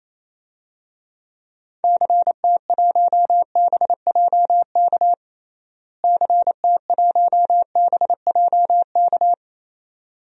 Morse code, heard as a keyed tone, about 28 wpm, 700 Hz. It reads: CT1BJK CT1BJK
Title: DXCC-100 Call Signs - 6 Characters - Repeated 2x - 28wpm